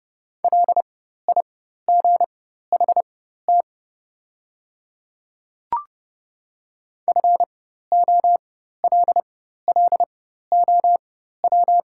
Morse code, heard as a keyed tone, about 30 wpm, spaced as for 14 wpm, 700 Hz. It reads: LIGHT FOLLOW
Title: 4x Character Spacing - Top 300 Words - 30wpm